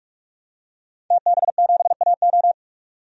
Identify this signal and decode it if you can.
Morse code, heard as a keyed tone, about 46 wpm, 700 Hz. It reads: TB7AO
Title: DXCC-100 Call Signs - 5 Characters - 45wpm